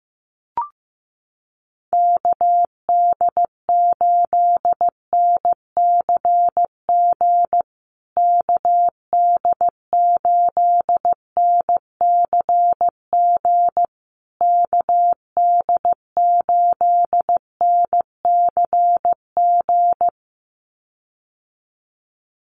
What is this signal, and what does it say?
Morse code, heard as a keyed tone, about 15 wpm, 700 Hz. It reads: KD8NCG KD8NCG KD8NCG
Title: USA Call Signs Repeated 3x 15wpm